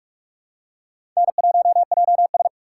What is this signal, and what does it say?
Morse code, heard as a keyed tone, about 46 wpm, 700 Hz. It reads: N1JS
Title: USA Call Signs - 4 Characters - 2x Character Spacing - 45wpm